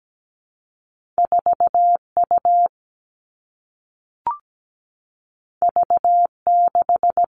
Morse code, heard as a keyed tone, about 17 wpm, 700 Hz. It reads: 4U V6
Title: Two Challenging Letter-Number Combinations 17wpm